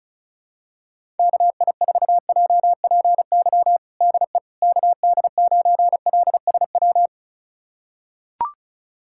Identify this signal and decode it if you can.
Morse code, heard as a keyed tone, about 36 wpm, 700 Hz. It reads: KI4JPY DE KD9LSW
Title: USA Callsign DE Callsign 35wpm